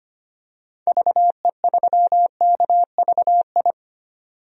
Morse code, heard as a keyed tone, about 25 wpm, 700 Hz. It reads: VE3KVI
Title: DXCC-100 Call Signs - 6 Characters - 25wpm